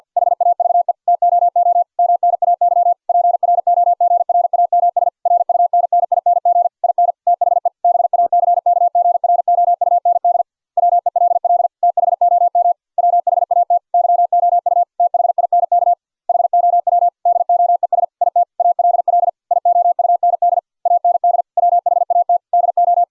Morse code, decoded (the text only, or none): HAVE TOO MNAY PROGRAMS RUNNING IN THE BACKGROUND WELL THOM WHAT YOU THINK HOW DOES IT ALL IOUND AND WHAT DO